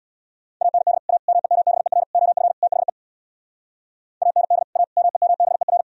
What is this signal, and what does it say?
Morse code, heard as a keyed tone, about 64 wpm, 700 Hz. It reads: AND A MEMBER OF THE AND A MEMBER